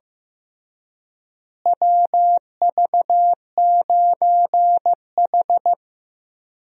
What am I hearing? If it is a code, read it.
Morse code, heard as a keyed tone, about 16 wpm, 700 Hz. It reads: WV9H
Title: USA Call Signs - 4 Characters - Repeated 3x - 15wpm